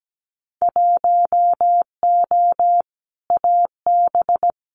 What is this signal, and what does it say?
Morse code, heard as a keyed tone, about 17 wpm, 700 Hz. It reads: 1O AB
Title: Winter Field Day Exchanges Encore 17wpm